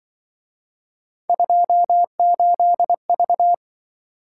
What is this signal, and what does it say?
Morse code, heard as a keyed tone, about 25 wpm, 700 Hz. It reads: 28V